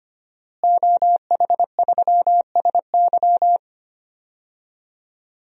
Morse code, heard as a keyed tone, about 25 wpm, 700 Hz. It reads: OH3SY
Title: DXCC-100 Call Signs - 5 Characters - Repeated 2x - 25wpm